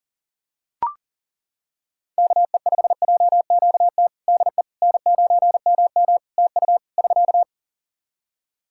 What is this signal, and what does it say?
Morse code, heard as a keyed tone, about 40 wpm, 700 Hz. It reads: KE5JQT DE N9MM TU <SK>
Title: K1USN SST QSO Practice 40wpm